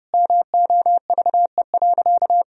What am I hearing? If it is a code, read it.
Morse code, heard as a keyed tone, about 30 wpm, 700 Hz. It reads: MOVE.
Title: Sentences from Top 1000 Words - Review Part 2 - 30wpm